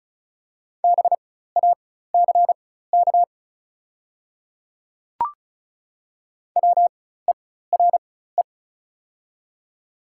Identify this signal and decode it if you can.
Morse code, heard as a keyed tone, about 36 wpm, spaced as for 16 wpm, 700 Hz. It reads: BACK WERE
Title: 4x Character Spacing - Top 300 Words - 35wpm